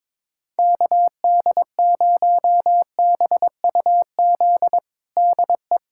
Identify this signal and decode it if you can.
Morse code, heard as a keyed tone, about 22 wpm, 700 Hz. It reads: KD0BUZ DE
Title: USA Callsign DE Callsign - 6 Characters - 22wpm